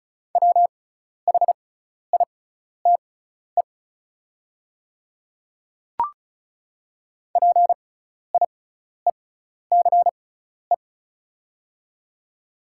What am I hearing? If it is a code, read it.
Morse code, heard as a keyed tone, about 36 wpm, spaced as for 12 wpm, 690 Hz. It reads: WHITE PIECE
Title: 6x Character Spacing - Top 500 Words - 35wpm